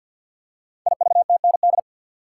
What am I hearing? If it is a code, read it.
Morse code, heard as a keyed tone, about 48 wpm, 700 Hz. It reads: IVTND
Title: Code Groups - 5 Letters - 50wpm